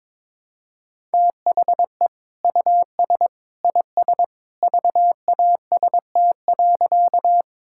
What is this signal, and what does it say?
Morse code, heard as a keyed tone, about 22 wpm, 700 Hz. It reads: THE US IS VAST.
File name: practice / 22wpm / Sentences with 2-Word Phrases - 22wpm